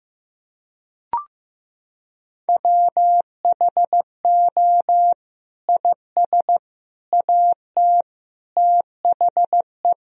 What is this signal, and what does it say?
Morse code, heard as a keyed tone, about 15 wpm, 700 Hz. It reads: WHO IS AT THE
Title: Sentences from Top 1000 Words - Review Part 4 - 15wpm